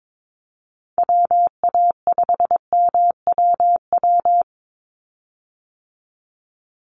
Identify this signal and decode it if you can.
Morse code, heard as a keyed tone, about 22 wpm, 700 Hz. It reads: WA5MWW
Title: USA Call Signs 22wpm